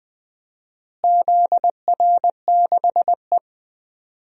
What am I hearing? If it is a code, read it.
Morse code, heard as a keyed tone, about 21 wpm, 700 Hz. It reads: ZR6E
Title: DXCC-100 Call Signs - 4 Characters - Repeated 3x - 20wpm